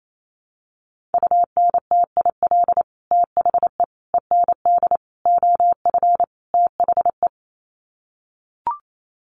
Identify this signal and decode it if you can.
Morse code, heard as a keyed tone, about 28 wpm, 700 Hz. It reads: UNTIL THE END OF THE
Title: Sets of 5 Words Encore 28wpm